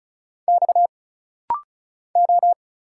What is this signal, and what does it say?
Morse code, heard as a keyed tone, about 34 wpm, 700 Hz. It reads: X O
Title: Single Letters - Rapid-Fire 35wpm